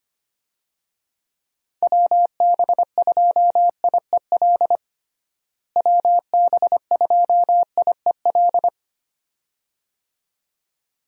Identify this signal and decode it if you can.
Morse code, heard as a keyed tone, about 25 wpm, 700 Hz. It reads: WB2IEL WB2IEL
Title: USA Call Signs - 6 Characters - Repeated 2x - 25wpm